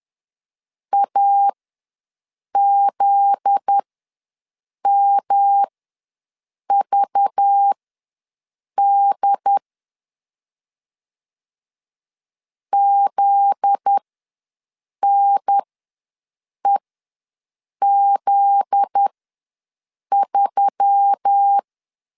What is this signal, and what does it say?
Morse code, heard as a keyed tone, about 10 wpm, spaced as for 6 wpm, 800 Hz. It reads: AZMVD ZNEZ3
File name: LEZIONI / 7 GF9Z / 7 GF9Z ws 10wpm 3919